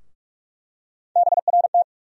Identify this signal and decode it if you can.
Morse code, heard as a keyed tone, about 42 wpm, 700 Hz. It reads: BRT
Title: USA Call Sign Postfixes - 3 Letters 45wpm